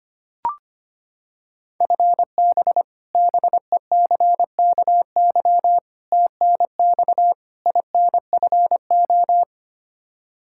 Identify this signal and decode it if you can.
Morse code, heard as a keyed tone, about 25 wpm, 700 Hz. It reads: FB BECKY TNX INFO
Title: Practice QSO - Exchange 3 - More Info - 25wpm